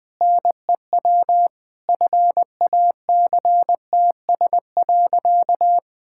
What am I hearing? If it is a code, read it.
Morse code, heard as a keyed tone, about 20 wpm, 700 Hz. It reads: NEW FACTS.
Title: Sentences with 5-Word Phrases - 20wpm